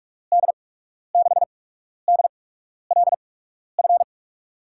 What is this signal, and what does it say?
Morse code, heard as a keyed tone, about 46 wpm, spaced as for 12 wpm, 700 Hz. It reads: D6DLF
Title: USA Call Signs - 6 Characters - 8x Character Spacing - 45wpm